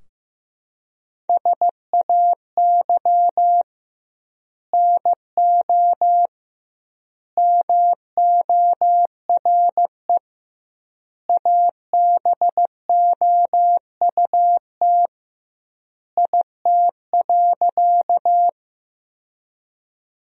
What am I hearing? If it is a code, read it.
Morse code, heard as a keyed tone, about 15 wpm, 700 Hz. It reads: SAY NO MORE ABOUT IT.
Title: 2x Word-Spacing - Sentences from Top 200 Words - 15wpm